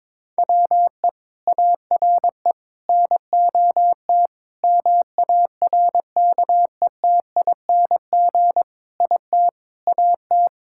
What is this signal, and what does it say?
Morse code, heard as a keyed tone, about 22 wpm, 700 Hz. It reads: WE ARE NOT MARKETING IT AT
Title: Sentences from Top 1000 Words - Review Part 5 - 22wpm